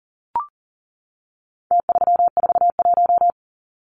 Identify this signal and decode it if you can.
Morse code, heard as a keyed tone, about 40 wpm, 700 Hz. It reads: T341
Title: Numbers - Contest Style - 4-Digit 40wpm